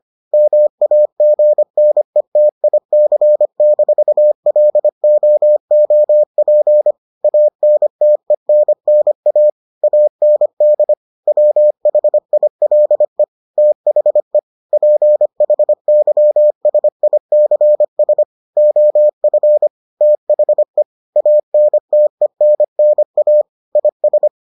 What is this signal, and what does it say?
Morse code, heard as a keyed tone, about 25 wpm, 600 Hz. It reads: MAGNETIC-LOOP ANTENNA AND WHILE THE PHYSICS OF THE ANTENNA IS